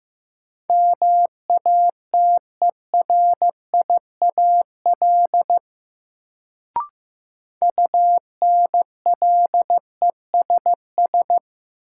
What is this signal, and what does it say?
Morse code, heard as a keyed tone, about 15 wpm, 700 Hz. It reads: MATERIAL UNLESS
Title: Sentences from Top 1000 Words - Review Part 5 - New Word Drill - 15wpm